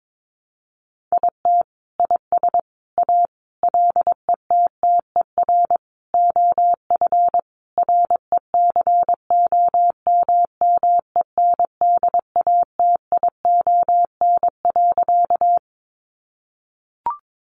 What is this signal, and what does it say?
Morse code, heard as a keyed tone, about 22 wpm, 700 Hz. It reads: IT IS A LETTER OF RECOMMENDATION.
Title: Sentences from Top 1400 Words 22wpm